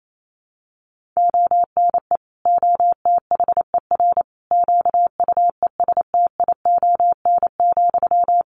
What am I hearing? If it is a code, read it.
Morse code, heard as a keyed tone, about 28 wpm, 700 Hz. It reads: ONE OTHER QUESTION,
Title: Sentences from Top 400 Words 28wpm